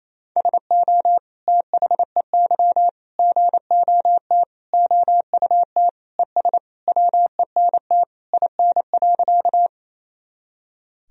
Morse code, heard as a keyed tone, about 28 wpm, 700 Hz. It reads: SO THEY GOT OUT ES WENT IN.
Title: Sentences from Top 300 Words and Abbreviations 28wpm